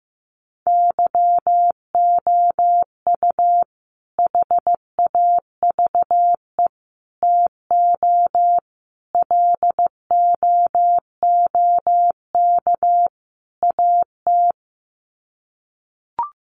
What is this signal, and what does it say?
Morse code, heard as a keyed tone, about 14 wpm, 700 Hz. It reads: YOU HAVE TO LOOK AT
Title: Sets of 5 Words Encore 15wpm